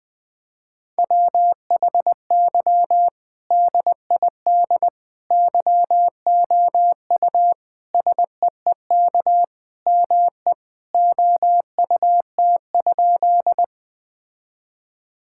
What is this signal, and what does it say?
Morse code, heard as a keyed tone, about 20 wpm, 700 Hz. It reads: WHY DID YOU SEEK ME OUT?
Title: Sentences from Top 1600 Words 20wpm